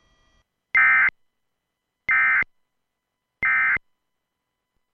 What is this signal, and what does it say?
Emergency Alert System (SAME) tones, heard as harsh buzzing bursts. It is heard from 0.7 to 3.8 s.